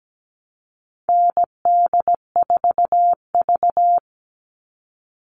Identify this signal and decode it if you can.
Morse code, heard as a keyed tone, about 17 wpm, 700 Hz. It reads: ND4V
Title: All Cut Numbers - 4-Digit 17wpm